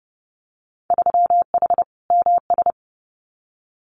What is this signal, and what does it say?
Morse code, heard as a keyed tone, about 32 wpm, 700 Hz. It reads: 3H MS
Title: Winter Field Day Exchanges Encore 30wpm